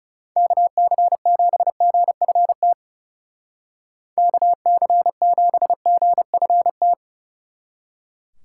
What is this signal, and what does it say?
Morse code, heard as a keyed tone, about 38 wpm, 700 Hz. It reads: KC7GFT KC7GFT